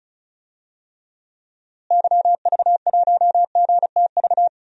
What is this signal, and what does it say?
Morse code, heard as a keyed tone, about 36 wpm, 700 Hz. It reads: YV1GTV
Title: DXCC-100 Call Signs - 6 Characters - Repeated 2x - 35wpm